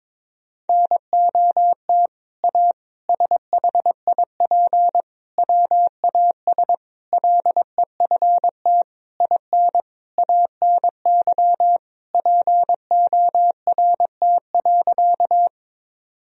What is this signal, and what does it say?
Morse code, heard as a keyed tone, about 22 wpm, 700 Hz. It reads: NOT A SHIP WAS LEFT IN ANY PORT.
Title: Sentences from Top 1000 Words - Review Part 2 - 22wpm